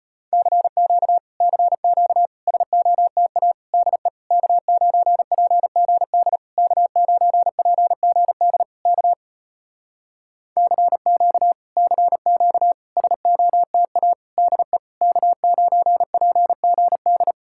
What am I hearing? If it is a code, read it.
Morse code, heard as a keyed tone, about 34 wpm, 700 Hz. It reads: CQ CQ SOTA DE K9PGD K9PGD K CQ CQ SOTA DE K9PGD